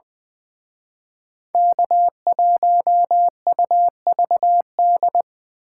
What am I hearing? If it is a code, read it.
Morse code, heard as a keyed tone, about 24 wpm, 700 Hz. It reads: K1UVD
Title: Speed Racing - USA 1x3 Call Signs 20wpm